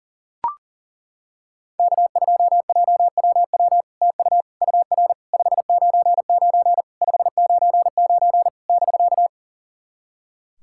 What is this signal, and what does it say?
Morse code, heard as a keyed tone, about 40 wpm, 700 Hz. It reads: K2JWW TU UR 599 599 <BK>
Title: SOTA QSO Practice 40wpm